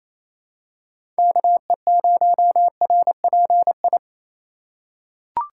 Morse code, heard as a keyed tone, about 28 wpm, 700 Hz. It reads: KE0RPI
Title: USA Call Signs - 6 Characters - 4x Character Spacing - 28wpm